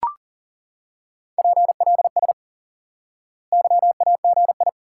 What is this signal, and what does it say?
Morse code, heard as a keyed tone, about 38 wpm, 700 Hz. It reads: PLS YAGI
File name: IQE Course 3 - Lesson 89 - YAGI - Two Elements - 40wpm